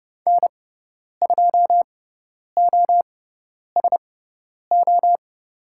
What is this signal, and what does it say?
Morse code, heard as a keyed tone, about 30 wpm, spaced as for 10 wpm, 700 Hz. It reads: N2OSO
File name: Lesson 16g - Letter c - Call Signs 30wpm